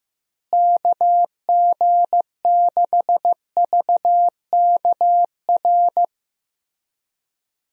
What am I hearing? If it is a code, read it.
Morse code, heard as a keyed tone, about 15 wpm, 700 Hz. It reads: KG6VKR
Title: USA Call Signs - 6 Characters - Repeated 3x - 15wpm